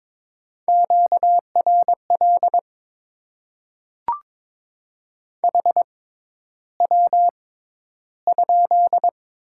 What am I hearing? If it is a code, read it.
Morse code, heard as a keyed tone, about 22 wpm, spaced as for 11 wpm, 700 Hz. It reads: QRL HW?
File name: Top 20 QSO Elements - 6x Character Spacing - 22wpm